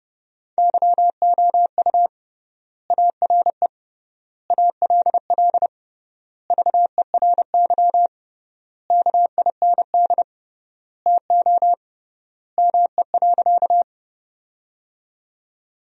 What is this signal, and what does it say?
Morse code, heard as a keyed tone, about 32 wpm, 700 Hz. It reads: YOU ARE ALL VERY KIND TO ME.
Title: 3x Word-Spacing - Sentences from Top 200 Words - 30wpm